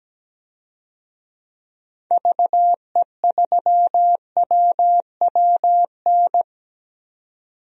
Morse code, heard as a keyed tone, about 17 wpm, 700 Hz. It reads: VE3WWN
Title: DXCC-100 Call Signs - 6 Characters - 17wpm